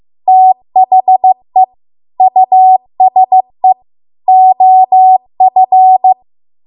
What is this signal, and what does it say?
Morse code, heard as a keyed tone, about 15 wpm, 750 Hz. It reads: THE USE OF
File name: Archive / 15 WPM / 15WPM